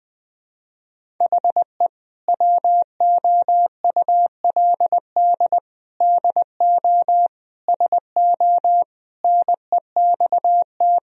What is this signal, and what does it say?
Morse code, heard as a keyed tone, about 19 wpm, 700 Hz. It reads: HE WOULD DO SO NEXT
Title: Sentences from Top 1000 Words - Review Part 1 - 20wpm